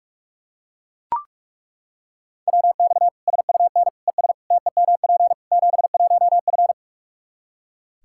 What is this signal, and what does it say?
Morse code, heard as a keyed tone, about 42 wpm, 700 Hz. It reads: WX SUN ES TEMP 71F
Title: Practice QSO - Exchange 2 - Info - 45wpm